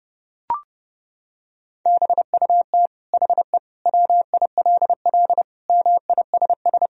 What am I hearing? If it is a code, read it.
Morse code, heard as a keyed tone, about 30 wpm, 700 Hz. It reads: BUT HE WILL MISS
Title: Sentences from Top 1000 Words - Review Part 3 - 30wpm